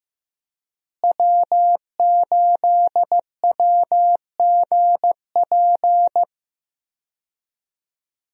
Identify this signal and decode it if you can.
Morse code, heard as a keyed tone, about 15 wpm, 700 Hz. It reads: W8WGP
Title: USA Call Signs - 5 Characters - Repeated 3x - 15wpm